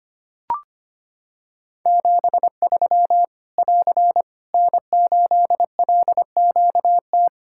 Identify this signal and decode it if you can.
Morse code, heard as a keyed tone, about 25 wpm, 700 Hz. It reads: 73 <AR> N8LQT
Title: K1USN SST QSO Practice 25wpm